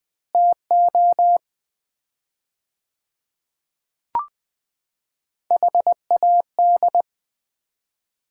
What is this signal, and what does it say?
Morse code, heard as a keyed tone, about 20 wpm, 700 Hz. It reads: TO HAD
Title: 4x Word-Spacing - Sets of 2 Words - 20wpm